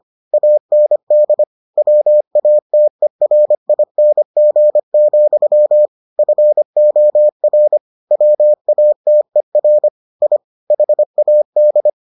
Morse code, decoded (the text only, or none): AND WATERING, FOR WATER I HAD